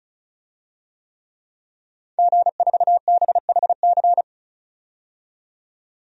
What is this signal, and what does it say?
Morse code, heard as a keyed tone, about 34 wpm, 700 Hz. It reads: G4BHC
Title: DXCC-100 Call Signs - 5 Characters - 35wpm